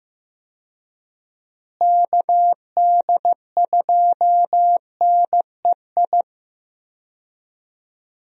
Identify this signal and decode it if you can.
Morse code, heard as a keyed tone, about 15 wpm, 700 Hz. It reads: KD2NEI
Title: USA Call Signs - 6 Characters - Repeated 2x - 15wpm